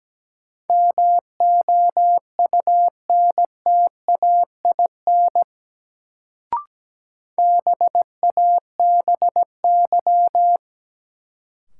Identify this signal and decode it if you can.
Morse code, heard as a keyed tone, about 17 wpm, 700 Hz. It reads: MOUNTAIN BABY
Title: Sentences from Top 1300 Words - New Word Drill Alt - 17wpm